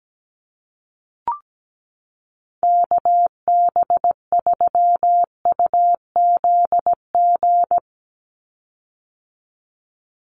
Morse code, heard as a keyed tone, about 17 wpm, 700 Hz. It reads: KB3UZG
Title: USA Call Signs 17wpm